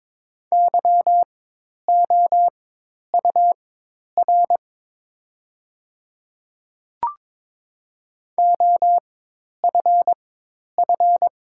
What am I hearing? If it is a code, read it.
Morse code, heard as a keyed tone, about 22 wpm, spaced as for 10 wpm, 700 Hz. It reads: YOUR OFF
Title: 4x Character Spacing - Top 200 Words - 22wpm